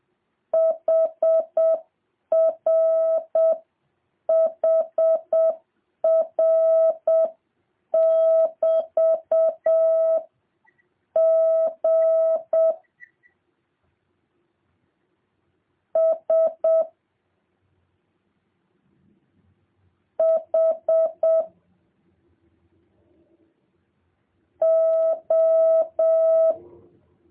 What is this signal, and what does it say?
Morse code, heard as a keyed tone, about 7 wpm, 650 Hz. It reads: HR HR <BT> G S H O